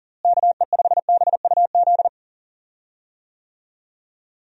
Morse code, heard as a keyed tone, about 40 wpm, 700 Hz. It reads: KE5BUZ
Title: USA Call Signs - 6 Characters - Repeated 2x - 40wpm